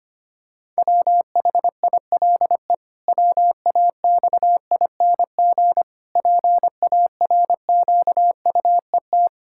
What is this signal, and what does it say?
Morse code, heard as a keyed tone, about 25 wpm, 700 Hz. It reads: WHILE WAXING PARQUET
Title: Pangram Challenge v2 - 25wpm